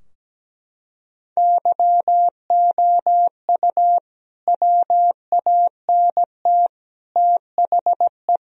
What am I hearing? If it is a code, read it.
Morse code, heard as a keyed tone, about 17 wpm, 700 Hz. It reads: YOU WANT THE
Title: Sentences from Top 1000 Words - Review Part 4 - 17wpm